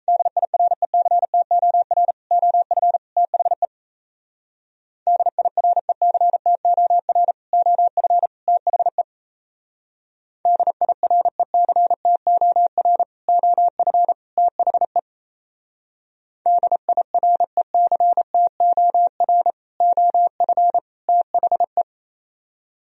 Morse code, heard as a keyed tone, about 36 wpm, 700 Hz. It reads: DIRECTOR OF THE DIRECTOR OF THE DIRECTOR OF THE DIRECTOR OF THE